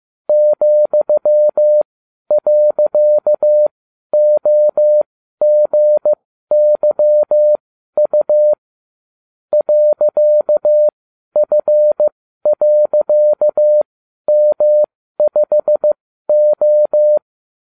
Morse code, heard as a keyed tone, about 15 wpm, spaced as for 12 wpm, 600 Hz. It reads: ,.OGYU .F.M5O